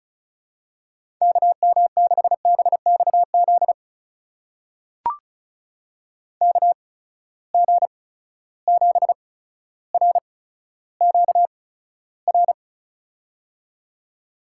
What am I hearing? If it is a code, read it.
Morse code, heard as a keyed tone, about 36 wpm, spaced as for 15 wpm, 700 Hz. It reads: KM6BXZ KG7RQR
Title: USA Call Signs - 6 Characters - 8x Character Spacing - 35wpm